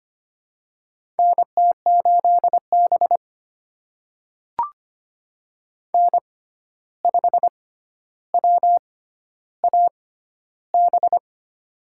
Morse code, heard as a keyed tone, about 25 wpm, spaced as for 12 wpm, 700 Hz. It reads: NT8B N5WAB
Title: USA Call Signs - 6x Character Spacing - 25wpm